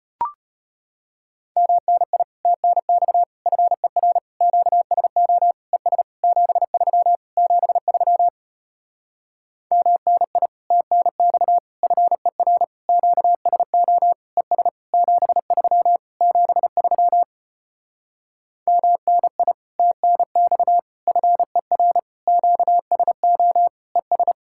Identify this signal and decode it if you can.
Morse code, heard as a keyed tone, about 33 wpm, 700 Hz. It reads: MNI TNX FER QSO ES 73 73 MNI TNX FER QSO ES 73 73 MNI TNX FER QSO ES